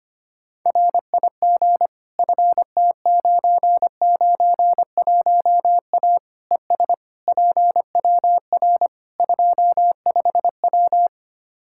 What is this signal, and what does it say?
Morse code, heard as a keyed tone, about 25 wpm, 700 Hz. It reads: RIG FT991A ES PWR 25W
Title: Practice QSO - Exchange 2 - Info - 25wpm